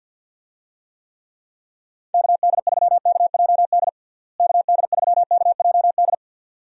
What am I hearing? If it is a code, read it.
Morse code, heard as a keyed tone, about 51 wpm, 700 Hz. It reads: KD3KJD KD3KJD
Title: USA Call Signs - 6 Characters - Repeated 2x - 50wpm